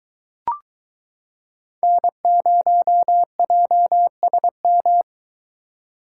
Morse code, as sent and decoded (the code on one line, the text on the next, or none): -. ----- .--- ... --
N0JSM